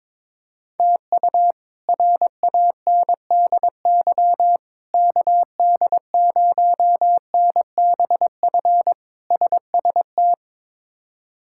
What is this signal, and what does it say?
Morse code, heard as a keyed tone, about 22 wpm, 700 Hz. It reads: TU RANDY KD0NBF SST
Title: K1USN SST QSO Practice - 22wpm